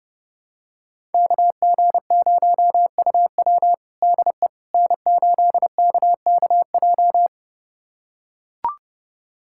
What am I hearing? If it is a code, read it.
Morse code, heard as a keyed tone, about 30 wpm, 700 Hz. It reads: KG0UW DE N8KKJ
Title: USA Callsign DE Callsign - 5 Characters - 30wpm